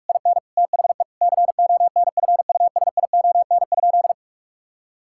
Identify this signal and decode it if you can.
Morse code, heard as a keyed tone, about 46 wpm, 700 Hz. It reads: IN THE CONFUSION?
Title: Sentences from Top 1600 Words 45wpm